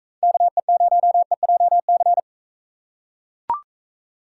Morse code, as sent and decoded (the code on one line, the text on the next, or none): -.- . ----- . .--- -.-.
KE0EJC